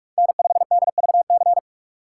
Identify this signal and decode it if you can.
Morse code, heard as a keyed tone, about 46 wpm, 700 Hz. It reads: N5DVC
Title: USA Call Signs - 5 Characters - 2x Character Spacing - 45wpm